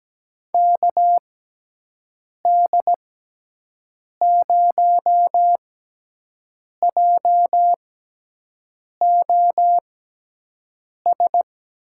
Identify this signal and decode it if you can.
Morse code, heard as a keyed tone, about 17 wpm, spaced as for 6 wpm, 700 Hz. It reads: KD0JOS
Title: USA Call Signs - 6 Characters - 6x Character Spacing - 17wpm